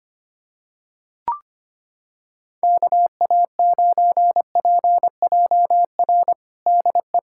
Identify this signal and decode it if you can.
Morse code, heard as a keyed tone, about 25 wpm, 700 Hz. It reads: KA9PJR DE
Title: USA Callsign DE Callsign 25wpm